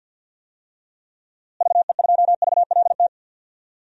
Morse code, heard as a keyed tone, about 50 wpm, 700 Hz. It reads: VE2VLT
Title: DXCC-100 Call Signs - 6 Characters - Repeated 2x - 50wpm